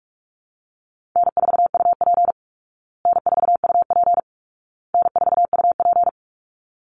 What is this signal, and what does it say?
Morse code, heard as a keyed tone, about 44 wpm, 700 Hz. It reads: N4UP N4UP N4UP
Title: Call Signs Repeated 2x - 45wpm